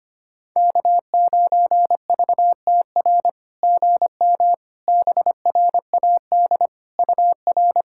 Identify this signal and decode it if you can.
Morse code, heard as a keyed tone, about 25 wpm, 700 Hz. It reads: K9VTR GM BRAD UR
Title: SOTA QSO Practice 25wpm